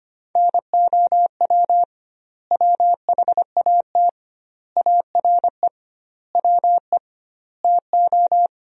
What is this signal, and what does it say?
Morse code, heard as a keyed tone, about 25 wpm, 700 Hz. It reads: NOW WHAT ARE WE TO
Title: 2x Word-Spacing - Sentences from Top 100 Words - 25wpm